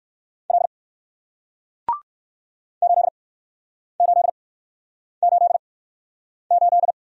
Morse code, heard as a keyed tone, about 57 wpm, spaced as for 9 wpm, 700 Hz. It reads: L 8888